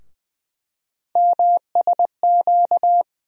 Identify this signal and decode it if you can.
Morse code, heard as a keyed tone, about 20 wpm, 700 Hz. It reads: MSQ
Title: USA Call Sign Postfixes - 3 Letters 20wpm